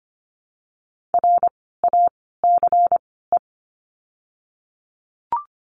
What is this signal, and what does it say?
Morse code, heard as a keyed tone, about 25 wpm, spaced as for 16 wpm, 700 Hz. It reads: RACE
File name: course / 25:10wpm / 094 Lesson 16f - Letter c - Words 25wpm